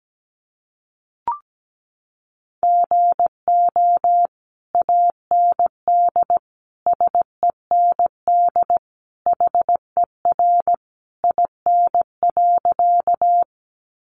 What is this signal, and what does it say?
Morse code, heard as a keyed tone, about 17 wpm, 700 Hz. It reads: GO AND SEND HER IN.
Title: Sentences from Top 1500 Words 17wpm